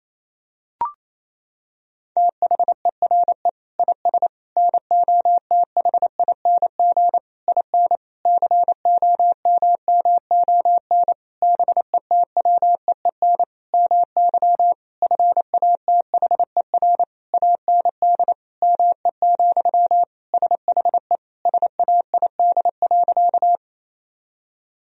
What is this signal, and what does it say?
Morse code, heard as a keyed tone, about 27 wpm, 700 Hz. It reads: THERE IS NOTHING IN COMMON BETWEEN MY FATHER AND ME, SHE SAID.
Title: Sentences from Top 1000 Words - Review Part 3 - 28wpm